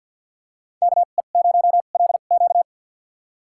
Morse code, heard as a keyed tone, about 46 wpm, spaced as for 35 wpm, 700 Hz. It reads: KE0LQ
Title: USA Call Signs - 5 Characters - 2x Character Spacing - 50wpm